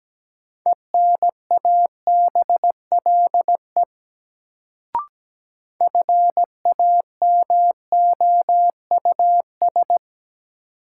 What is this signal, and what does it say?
Morse code, heard as a keyed tone, about 17 wpm, 700 Hz. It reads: ENABLE FAMOUS
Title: Sentences from Top 1500 Words - New Word Drill - 17wpm